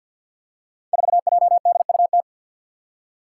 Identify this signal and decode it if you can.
Morse code, heard as a keyed tone, about 49 wpm, 700 Hz. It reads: 4JDUT
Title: Five Challenging Letter-Number Combinations 50wpm